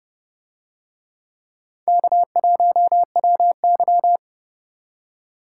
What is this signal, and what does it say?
Morse code, heard as a keyed tone, about 29 wpm, 700 Hz. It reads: K1WY